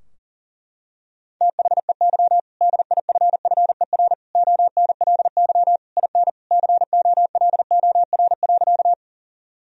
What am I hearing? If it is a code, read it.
Morse code, heard as a keyed tone, about 37 wpm, 700 Hz. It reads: THEY DIFFER ONLY IN COLOR.
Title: Sentences from Top 1000 Words - Review Part 2 - 40wpm